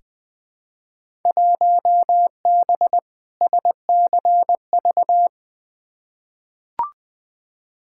Morse code, heard as a keyed tone, about 20 wpm, 700 Hz. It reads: 1B SCV
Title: Field Day Exchanges - 20wpm